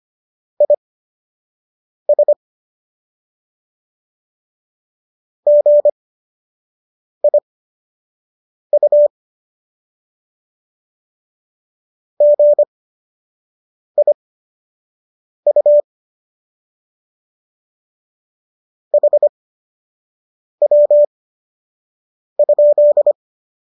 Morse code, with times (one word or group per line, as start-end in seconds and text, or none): 0.6-2.3 IS
5.5-9.1 GIU
12.2-15.8 GIU
18.9-23.1 HW?